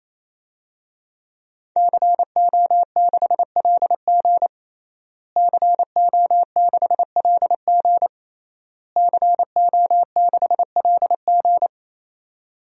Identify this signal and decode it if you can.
Morse code, heard as a keyed tone, about 28 wpm, 700 Hz. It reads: CO6LG CO6LG CO6LG
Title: DXCC-100 Call Signs - 5 Characters - Repeated 3x - 28wpm